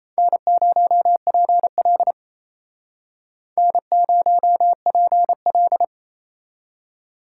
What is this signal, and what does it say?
Morse code, heard as a keyed tone, about 36 wpm, 700 Hz. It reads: N0PL N0PL